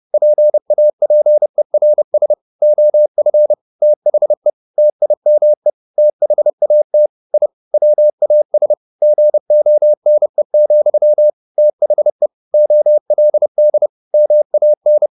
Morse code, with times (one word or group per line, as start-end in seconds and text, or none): 0.1-2.3 PAPERS
2.6-3.5 OF
3.8-4.5 THE
4.8-5.7 TIME
6.0-7.1 THAT
7.3-7.5 I
7.7-8.7 WAS
9.0-11.3 GONE,
11.6-12.3 THE
12.5-13.9 OLD
14.1-15.1 MAN